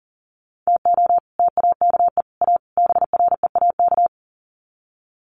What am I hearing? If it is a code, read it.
Morse code, heard as a keyed tone, about 40 wpm, 700 Hz. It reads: TO TAKE A BREAK
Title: Sets of 4 Words 40wpm